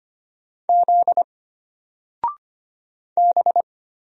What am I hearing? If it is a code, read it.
Morse code, heard as a keyed tone, about 25 wpm, 700 Hz. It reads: Z B
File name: Single-Letter-Number-25wpm